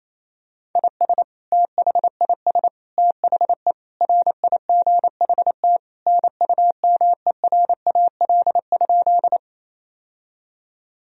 Morse code, heard as a keyed tone, about 28 wpm, 700 Hz. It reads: IS THIS THE RIGHT NUMERAL?
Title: Sentences from Top 1000 Words - Review Part 2 - 28wpm